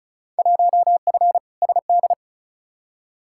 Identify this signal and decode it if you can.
Morse code, heard as a keyed tone, about 36 wpm, 700 Hz. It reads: 1F SD
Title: Field Day Exchanges - 35wpm